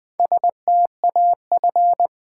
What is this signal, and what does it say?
Morse code, heard as a keyed tone, about 20 wpm, 700 Hz. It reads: STAF
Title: Build A Word - 1 Letter At A Time - 5-Letter Words - 20wpm